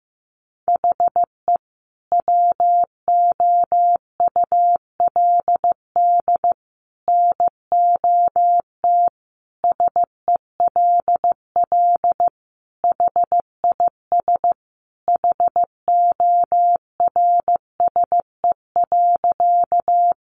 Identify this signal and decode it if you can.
Morse code, heard as a keyed tone, about 15 wpm, 700 Hz. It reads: HE WOULD NOT SELL HIS HORSE.
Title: Sentences from Top 1100 Words 15wpm